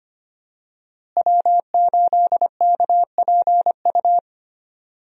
Morse code, heard as a keyed tone, about 25 wpm, 700 Hz. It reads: W8KPU
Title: USA Call Signs - 5 Characters - Repeated 2x - 25wpm